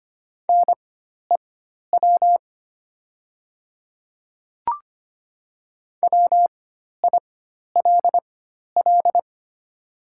Morse code, heard as a keyed tone, about 25 wpm, spaced as for 12 wpm, 700 Hz. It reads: NEW WILL
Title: 4x Character Spacing - Top 200 Words - 25wpm